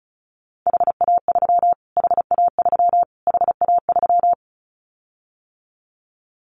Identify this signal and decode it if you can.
Morse code, heard as a keyed tone, about 34 wpm, 700 Hz. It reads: HA3 HA3 HA3
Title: Callsign Prefixes for Top 25 DXCC - Repeated 3x - 35wpm